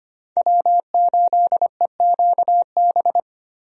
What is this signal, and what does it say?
Morse code, heard as a keyed tone, about 25 wpm, 700 Hz. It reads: W8EQB
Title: USA Call Signs - 5 Characters - Repeated 2x - 25wpm